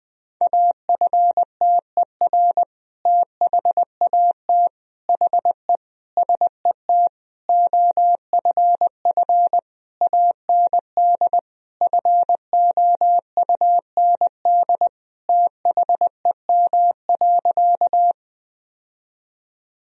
Morse code, heard as a keyed tone, about 20 wpm, 700 Hz. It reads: AFTER THAT HE SET OFF AND FOUND THEM.
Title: Sentences from Top 1000 Words - Review Part 1 - 20wpm